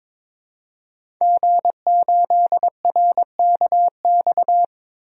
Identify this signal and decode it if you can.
Morse code, heard as a keyed tone, about 22 wpm, 700 Hz. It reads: G8RKX
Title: DXCC-100 Call Signs - 5 Characters - 22wpm